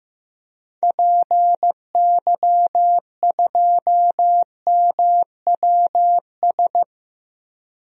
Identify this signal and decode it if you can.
Morse code, heard as a keyed tone, about 15 wpm, 700 Hz. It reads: PY2MWS
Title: DXCC-100 Call Signs - 6 Characters - Repeated 2x - 15wpm